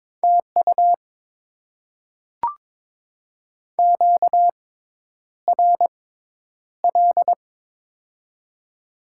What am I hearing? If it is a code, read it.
Morse code, heard as a keyed tone, about 21 wpm, spaced as for 10 wpm, 690 Hz. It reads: TU QRL
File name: Top 20 QSO Elements - 6x Character Spacing - 22wpm